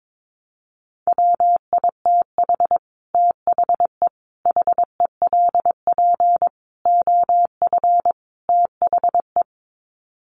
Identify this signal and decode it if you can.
Morse code, heard as a keyed tone, about 22 wpm, 700 Hz. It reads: WITH THE HELP OF THE
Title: Sets of 5 Words Encore 22wpm